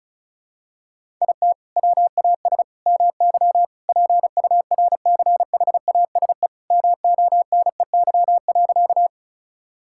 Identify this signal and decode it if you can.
Morse code, heard as a keyed tone, about 36 wpm, 700 Hz. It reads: IT WAS MY PURCHASE MONEY.
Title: Sentences from Top 1000 Words - Review Part 5 - 35wpm